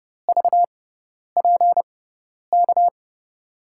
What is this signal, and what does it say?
Morse code, heard as a keyed tone, about 30 wpm, 700 Hz. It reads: V P K
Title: USA Call Signs - 6x Character Spacing - 30wpm